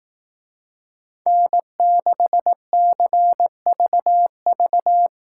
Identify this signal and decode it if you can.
Morse code, heard as a keyed tone, about 18 wpm, 700 Hz. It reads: N6CVV
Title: Speed Racing - USA Call Signs 15wpm